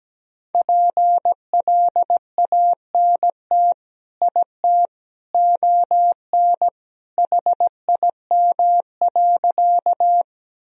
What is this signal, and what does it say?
Morse code, heard as a keyed tone, about 17 wpm, 700 Hz. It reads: PLANT IT ON HIM.
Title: Sentences from Top 1000 Words - Review Part 1 - 17wpm